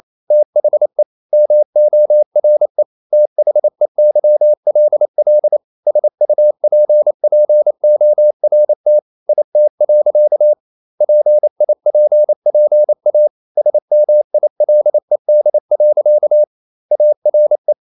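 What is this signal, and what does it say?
Morse code, heard as a keyed tone, about 28 wpm, 600 Hz. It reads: THE MORE THEYLL SUPPORT IT. PIPPA SMILED. ARE
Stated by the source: Morse code practice at 28 wpm, text of Ken Follett's novel Never in English, from CW med Gnister.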